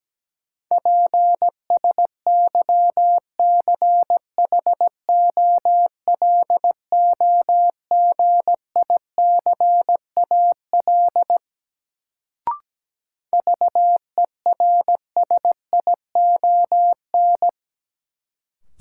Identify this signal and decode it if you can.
Morse code, heard as a keyed tone, about 17 wpm, 700 Hz. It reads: PSYCHOLOGICAL VERSION
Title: Sentences from Top 1400 Words - New Word Drill Alt - 17wpm